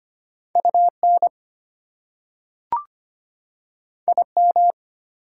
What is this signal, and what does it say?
Morse code, heard as a keyed tone, about 25 wpm, 700 Hz. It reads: UN IM
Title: Prefixes and Suffixes 25wpm